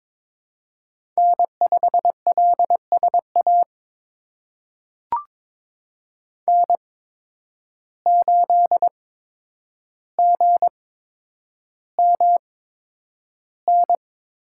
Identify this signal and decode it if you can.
Morse code, heard as a keyed tone, about 22 wpm, spaced as for 9 wpm, 700 Hz. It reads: N5LSA N8GMN
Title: USA Call Signs - 5 Characters - 8x Character Spacing - 22wpm